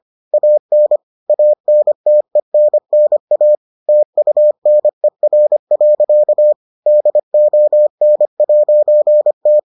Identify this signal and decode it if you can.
Morse code, heard as a keyed tone, about 25 wpm, 600 Hz. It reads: AN ANTENNA TUNER. DON'T